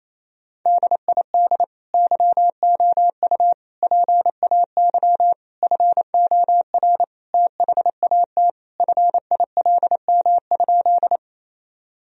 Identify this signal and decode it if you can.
Morse code, heard as a keyed tone, about 28 wpm, 700 Hz. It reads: DID YOU PAY FOR THAT FILM?
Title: Sentences from Top 1000 Words - Review Part 3 - 28wpm